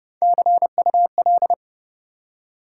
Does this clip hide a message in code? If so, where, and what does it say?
Yes: Morse code from 0.2 to 1.5 s, heard as a keyed tone, about 32 wpm, 700 Hz. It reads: CUL